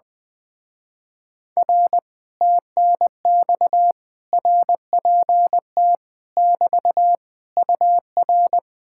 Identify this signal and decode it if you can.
Morse code, heard as a keyed tone, about 20 wpm, 700 Hz. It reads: R TNX RPT <BT> UR
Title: Speed-Racing Common QSO Phrases - 20wpm